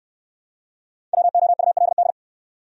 Morse code, heard as a keyed tone, about 58 wpm, 700 Hz. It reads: WOULD